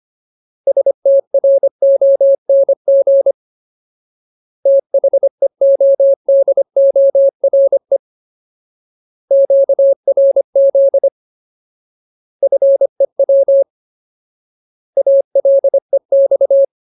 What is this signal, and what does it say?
Morse code, heard as a keyed tone, about 25 wpm, 550 Hz. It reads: STRONG THEODORE QRZ FEW ALEX
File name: Ultimate Word List / 25WPM Ultimate